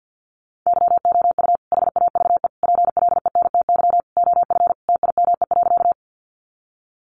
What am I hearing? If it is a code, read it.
Morse code, heard as a keyed tone, about 50 wpm, 700 Hz. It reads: YOU HAVE PLENTY OF TIME.
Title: Sentences from Top 1300 Words 50wpm